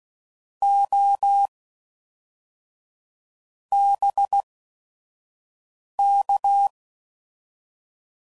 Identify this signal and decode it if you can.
Morse code, heard as a keyed tone, about 16 wpm, 800 Hz. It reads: O B K